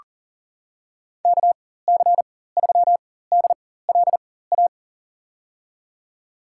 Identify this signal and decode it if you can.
Morse code, heard as a keyed tone, about 40 wpm, spaced as for 19 wpm, 700 Hz. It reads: KC3DLA
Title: USA Call Signs - 6 Characters - 4x Character Spacing - 40wpm